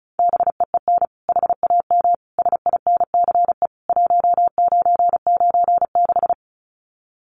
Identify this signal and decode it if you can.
Morse code, heard as a keyed tone, about 34 wpm, 700 Hz. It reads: BEEN HAM SINCE 1996
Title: Common QSO Phrases 35wpm